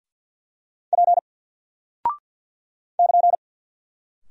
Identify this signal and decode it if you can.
Morse code, heard as a keyed tone, about 54 wpm, 700 Hz. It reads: P <KN>
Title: Single Character (Letter:Number:Punctuation:Prosign) Alt 50wpm